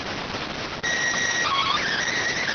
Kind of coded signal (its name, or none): SSTV (slow-scan television)